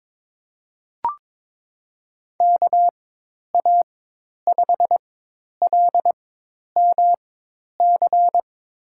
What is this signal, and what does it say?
Morse code, heard as a keyed tone, about 22 wpm, spaced as for 10 wpm, 700 Hz. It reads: KA5LMC
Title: USA Call Signs - 6 Characters - 4x Character Spacing - 22wpm